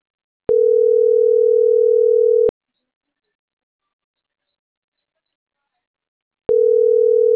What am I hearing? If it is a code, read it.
Ringback tone (North America), 440 + 480 Hz.